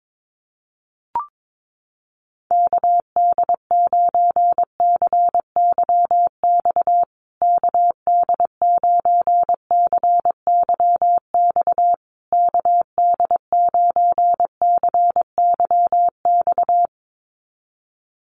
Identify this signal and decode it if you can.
Morse code, heard as a keyed tone, about 22 wpm, 700 Hz. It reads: KD9CYX KD9CYX KD9CYX
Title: USA Call Signs Repeated 3x 22wpm